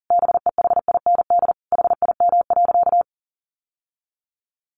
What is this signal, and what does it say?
Morse code, heard as a keyed tone, about 37 wpm, 700 Hz. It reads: BEHIND HIM.
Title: Sentences from Top 1000 Words 40wpm